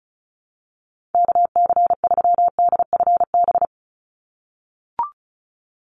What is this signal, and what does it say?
Morse code, heard as a keyed tone, about 34 wpm, 700 Hz. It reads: KC3DFB
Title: USA Call Signs 35wpm